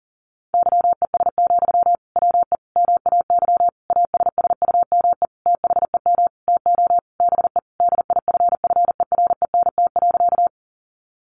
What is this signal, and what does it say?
Morse code, heard as a keyed tone, about 40 wpm, 700 Hz. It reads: YES, WE MAY ASSUME THEM TO BE DIFFERENT.
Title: Sentences from Top 1500 Words 40wpm